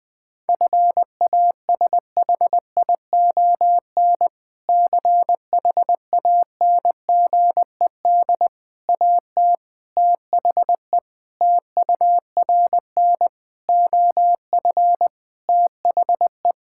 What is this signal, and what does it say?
Morse code, heard as a keyed tone, about 20 wpm, 700 Hz. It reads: FASHION CHANGED AT THE TURN OF THE
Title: Sentences with 5-Word Phrases - 20wpm